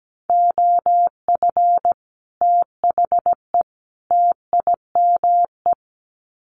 Morse code, heard as a keyed tone, about 17 wpm, 700 Hz. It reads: OF THE TIME
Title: Sets of 3 Words 17wpm